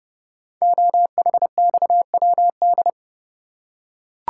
Morse code, heard as a keyed tone, about 30 wpm, 700 Hz. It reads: OHXWD
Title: Code Groups - 5 Letters - 30wpm